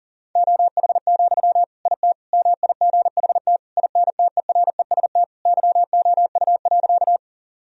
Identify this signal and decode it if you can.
Morse code, heard as a keyed tone, about 40 wpm, 700 Hz. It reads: OH, IT MIGHT INTEREST YOU.
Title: Sentences from Top 1000 Words - Review Part 2 - 40wpm